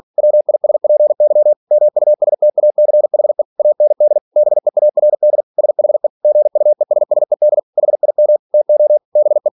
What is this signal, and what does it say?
Morse code, heard as a keyed tone, about 46 wpm, 600 Hz. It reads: WISPY MUSTACHE AND BEARD SHE GUESSED HIM TO BE
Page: Never 47wpm